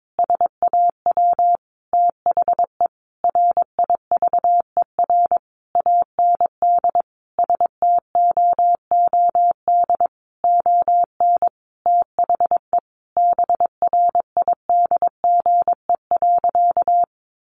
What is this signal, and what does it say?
Morse code, heard as a keyed tone, about 22 wpm, 700 Hz. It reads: SAW THE RIVER AND STOOD ON THE BRIDGE.
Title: Sentences from Top 1200 Words 22wpm